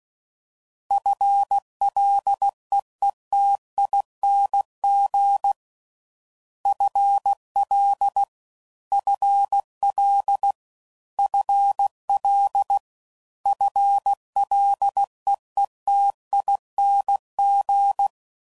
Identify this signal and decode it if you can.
Morse code, heard as a keyed tone, about 16 wpm, 800 Hz. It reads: FLEETING FL FL FL FLEETING